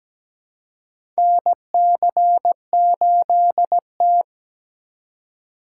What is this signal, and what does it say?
Morse code, heard as a keyed tone, about 17 wpm, 700 Hz. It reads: NC8T
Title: USA Call Signs - 4 Characters - 2x Character Spacing - 17wpm